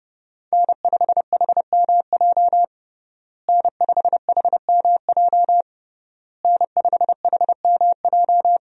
Morse code, heard as a keyed tone, about 30 wpm, 700 Hz. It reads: N5HMJ N5HMJ N5HMJ
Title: USA Call Signs - 5 Characters - Repeated 3x - 30wpm